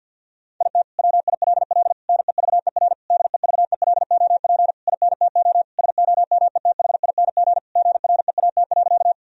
Morse code, heard as a keyed tone, about 50 wpm, 700 Hz. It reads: IT WILL NEVER DEVELOP INTO SOMETHING GREAT.
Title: Sentences from Top 1000 Words - Review Part 2 - 50wpm